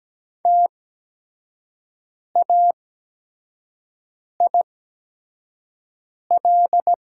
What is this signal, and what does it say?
Morse code, heard as a keyed tone, about 17 wpm, spaced as for 5 wpm, 700 Hz. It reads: TAIL
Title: 8x Character Spacing - Top 500 Words - 17wpm